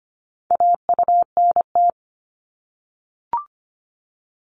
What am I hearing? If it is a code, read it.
Morse code, heard as a keyed tone, about 25 wpm, 700 Hz. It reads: AUNT
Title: Multi-letter Phonemes 25wpm